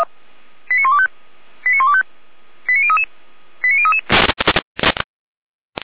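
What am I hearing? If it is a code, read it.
Five-tone selective call (ZVEI), heard as quick runs of five tones, tones of about 70 ms. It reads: 89125 89125 89033 89033
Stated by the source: ZVEI five-tone call 89062 received by SDR.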